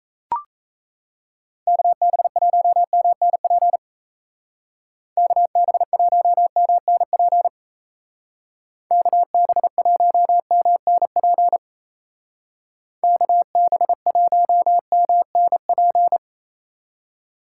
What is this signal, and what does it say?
Morse code, heard as a keyed tone, about 34 wpm, 700 Hz. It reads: KB1MNP KB1MNP KB1MNP KB1MNP
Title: Speed Racing - USA 2x3 Call Signs 28wpm